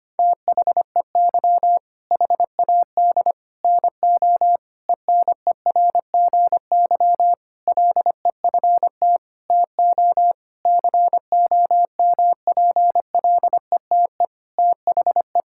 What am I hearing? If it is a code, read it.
Morse code, heard as a keyed tone, about 25 wpm, 700 Hz. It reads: THEY HAD NO ENERGY LEFT TO COMPLETE THE
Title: Sentences from Top 1000 Words - Review Part 6 - 25wpm